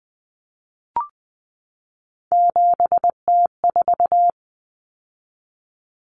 Morse code, heard as a keyed tone, about 19 wpm, 700 Hz. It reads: 7T4
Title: Common Cut Numbers - 3-Digit 20wpm